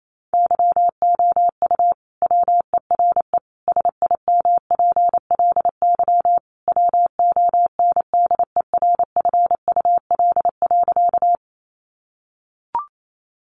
Morse code, heard as a keyed tone, about 28 wpm, 700 Hz. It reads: YOU WERE SIMPLY WONDERFUL.
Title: Sentences from Top 1100 Words 28wpm